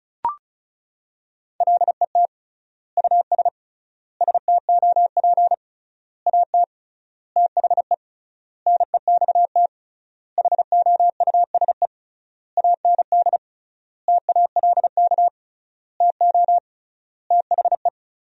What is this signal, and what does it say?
Morse code, heard as a keyed tone, about 36 wpm, 700 Hz. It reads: LET US STOP AT THE NEXT HOUSE AND TALK TO THE
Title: 3x Word-Spacing - Sentences from Top 400 Words - 35wpm